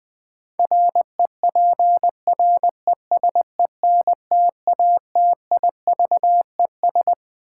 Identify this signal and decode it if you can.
Morse code, heard as a keyed tone, about 20 wpm, 700 Hz. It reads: REPRESENTATIVES
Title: Sets of 5 Words Encore - New Word Drill - 20wpm